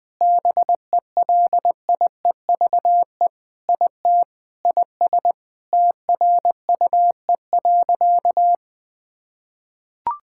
Morse code, heard as a keyed tone, about 21 wpm, 700 Hz. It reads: BELIEVE IT IS TRUE.
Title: Sentences with 5-Word Phrases - 20wpm